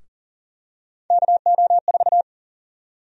Morse code, heard as a keyed tone, about 39 wpm, 700 Hz. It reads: KO4
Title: USA Call Sign Prefixes - 40wpm